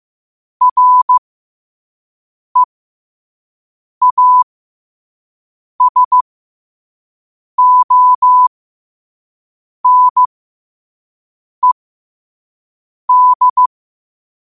Morse code, heard as a keyed tone, about 15 wpm, spaced as for 5 wpm, 1000 Hz. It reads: REASONED
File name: qotdfiles / 5wpm